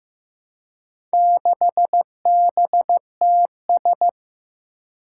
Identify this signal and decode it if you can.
Morse code, heard as a keyed tone, about 15 wpm, 700 Hz. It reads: 6BTS
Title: Four Challenging Letter-Number Combinations 15wpm